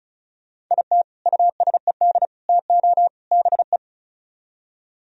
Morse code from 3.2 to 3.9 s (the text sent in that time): BE